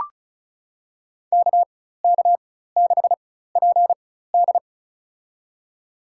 Morse code, heard as a keyed tone, about 34 wpm, spaced as for 16 wpm, 700 Hz. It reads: KK6PD
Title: USA Call Signs - 5 Characters - 4x Character Spacing - 35wpm